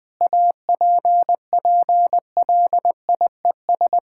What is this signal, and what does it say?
Morse code, heard as a keyed tone, about 21 wpm, 700 Hz. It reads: APPLIES
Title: Sentences with 5-Word Phrases - 20wpm